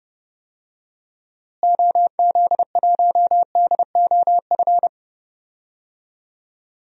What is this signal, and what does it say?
Morse code, heard as a keyed tone, about 32 wpm, 700 Hz. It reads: OZ1DOF
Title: DXCC-100 Call Signs - 6 Characters - Repeated 2x - 30wpm